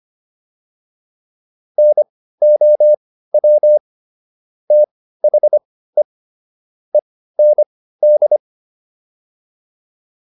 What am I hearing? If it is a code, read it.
Morse code, heard as a keyed tone, about 25 wpm, spaced as for 15 wpm, 600 Hz. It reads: NOW THE END